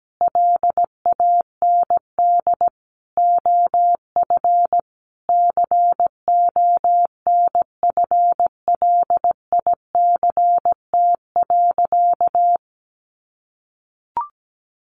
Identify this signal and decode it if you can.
Morse code, heard as a keyed tone, about 17 wpm, 700 Hz. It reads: LAND OF CONFLICT.
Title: Sentences from Top 1300 Words 17wpm